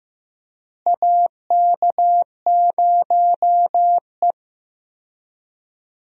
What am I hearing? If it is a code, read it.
Morse code, heard as a keyed tone, about 15 wpm, 700 Hz. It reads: AK0E
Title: USA Call Signs - 4 Characters - Repeated 3x - 15wpm